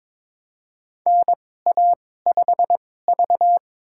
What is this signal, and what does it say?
Morse code, heard as a keyed tone, about 22 wpm, spaced as for 16 wpm, 700 Hz. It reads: NA5V
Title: USA Call Signs - 4 Characters - 2x Character Spacing - 22wpm